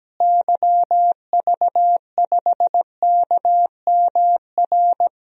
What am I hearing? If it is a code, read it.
Morse code, heard as a keyed tone, about 17 wpm, 700 Hz. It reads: YV5KMR
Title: DXCC-100 Call Signs - 6 Characters - Repeated 3x - 17wpm